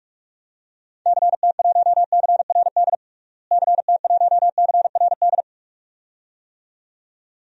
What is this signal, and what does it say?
Morse code, heard as a keyed tone, about 47 wpm, 700 Hz. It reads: CT1CRD CT1CRD
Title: DXCC-100 Call Signs - 6 Characters - Repeated 2x - 45wpm